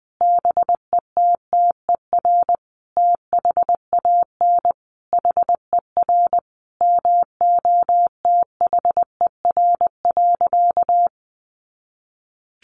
Morse code, heard as a keyed tone, about 20 wpm, 700 Hz. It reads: BETTER THAN HER MOTHER.
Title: Sentences from Top 400 Words 20wpm